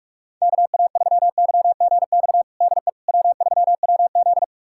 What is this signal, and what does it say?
Morse code, heard as a keyed tone, about 46 wpm, 700 Hz. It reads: KA3YGX DE W3WZ
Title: USA Callsign DE Callsign 45wpm